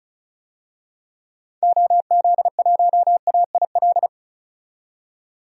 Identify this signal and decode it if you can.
Morse code, heard as a keyed tone, about 36 wpm, 700 Hz. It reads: OZ1AIL
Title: DXCC-100 Call Signs - 6 Characters - 35wpm